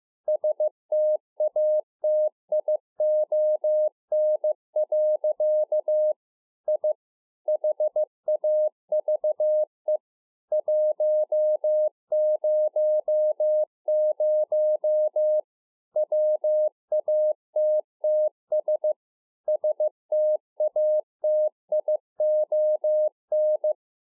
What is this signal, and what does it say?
Morse code, heard as a keyed tone, about 15 wpm, 610 Hz. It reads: STATION. I HAVE 100 WATTS STATION